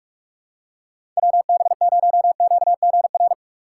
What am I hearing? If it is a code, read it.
Morse code, heard as a keyed tone, about 46 wpm, 700 Hz. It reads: WB0QGR
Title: USA Call Signs - 6 Characters - 3x Character Spacing - 45wpm